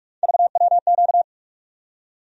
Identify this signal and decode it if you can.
Morse code, heard as a keyed tone, about 46 wpm, 700 Hz. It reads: VWQ
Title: USA Call Sign Postfixes - 3 Letters 45wpm